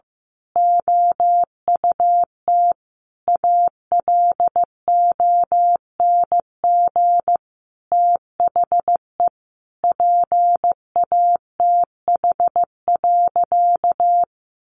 Morse code, heard as a keyed tone, about 15 wpm, 700 Hz. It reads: OUT ALONG THE PATH.
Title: A Sound Of Thunder 15wpm